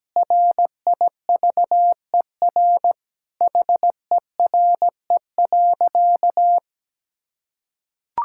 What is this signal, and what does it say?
Morse code, heard as a keyed tone, about 17 wpm, 700 Hz. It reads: RIVER HERE.
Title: Sentences from Top 1000 Words - Review Part 5 - 17wpm